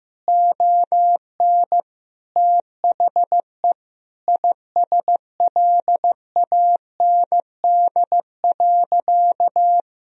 Morse code, heard as a keyed tone, about 16 wpm, 700 Hz. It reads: ON THE ISLAND.
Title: Sentences from Top 1000 Words - Review Part 3 - 15wpm